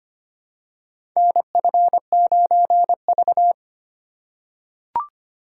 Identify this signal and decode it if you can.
Morse code, heard as a keyed tone, about 25 wpm, 700 Hz. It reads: NF9V
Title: USA Call Signs - 8x Character Spacing - 25wpm